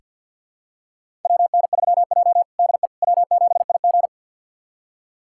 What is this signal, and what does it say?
Morse code, heard as a keyed tone, about 46 wpm, 700 Hz. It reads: WN3J DE W7IG
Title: USA Callsign DE Callsign - 4 Characters - 50wpm